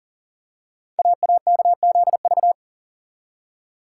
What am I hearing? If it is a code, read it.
Morse code, heard as a keyed tone, about 41 wpm, 700 Hz. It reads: AAKZV
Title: Code Groups - 5 Letter:Numbers - 40wpm